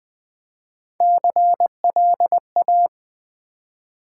Morse code, heard as a keyed tone, about 20 wpm, 700 Hz. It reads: CLA
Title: Build A Word - 3 Letters At A Time - 8-Letter Words - 20wpm